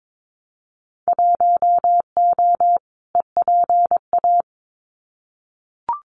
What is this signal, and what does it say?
Morse code, heard as a keyed tone, about 22 wpm, 700 Hz. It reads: 1O EPA
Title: Winter Field Day Exchanges 22wpm